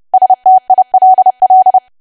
Morse code, heard as a keyed tone, about 30 wpm, 750 Hz. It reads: STILL
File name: cwa / W1AW-Level3 / 30WPM